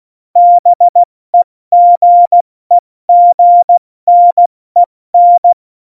Morse code, heard as a keyed tone, about 16 wpm, 700 Hz. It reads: BEGEGNEN
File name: Morsekurs / Witze / Witz 10.Juli 2018, 70 F80